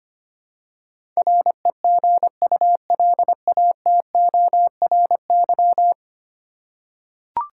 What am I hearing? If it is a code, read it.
Morse code, heard as a keyed tone, about 25 wpm, 700 Hz. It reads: REGULATORY
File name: Ten Letter Words - 25wpm